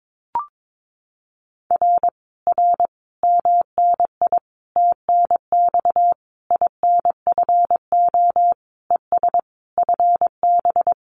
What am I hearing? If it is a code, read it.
Morse code, heard as a keyed tone, about 22 wpm, 700 Hz. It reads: R R MNI TNX INFO ES FB
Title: Common QSO Phrases Encore 22wpm